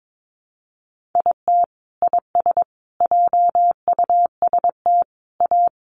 Morse code, heard as a keyed tone, about 22 wpm, 700 Hz. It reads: IT IS JUST A
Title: Sentences from Top 1200 Words 22wpm